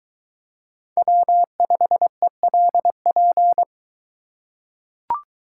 Morse code, heard as a keyed tone, about 21 wpm, 700 Hz. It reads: W5ELP